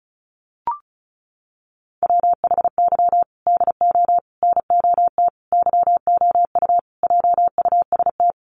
Morse code, heard as a keyed tone, about 34 wpm, 700 Hz. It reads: WHY DO NOT YOU JUST
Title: Sets of 5 Words 35wpm